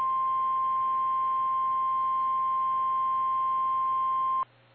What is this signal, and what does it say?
Emergency Alert System (SAME) tones (the 1050 Hz attention tone). It is heard from 0.2 to 4.2 s.
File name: eas / KKPRKXPN / KKPRKXPN